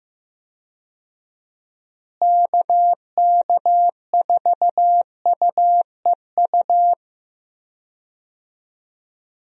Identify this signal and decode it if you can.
Morse code, heard as a keyed tone, about 15 wpm, 700 Hz. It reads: KK4UEU
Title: USA Call Signs - 6 Characters - Repeated 2x - 15wpm